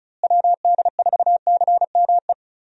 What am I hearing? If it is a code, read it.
Morse code, heard as a keyed tone, about 34 wpm, 700 Hz. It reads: WD4CME
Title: USA Call Signs - 6 Characters - Repeated 3x - 35wpm